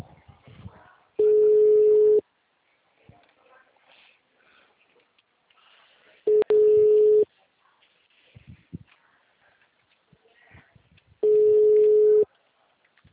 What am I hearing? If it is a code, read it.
Ringback tone (Europe, 425 Hz).
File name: out- -2013- - - .47